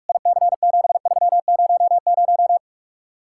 Morse code, heard as a keyed tone, about 44 wpm, 700 Hz. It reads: IC7300